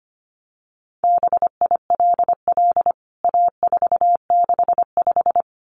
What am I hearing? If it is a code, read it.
Morse code, heard as a keyed tone, about 25 wpm, 700 Hz. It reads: BILL A465
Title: CWT Exchange 25wpm